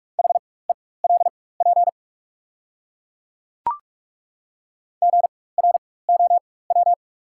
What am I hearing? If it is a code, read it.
Morse code, heard as a keyed tone, about 42 wpm, spaced as for 21 wpm, 690 Hz. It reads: HELP GROW
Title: 4x Character Spacing - Top 300 Words - 45wpm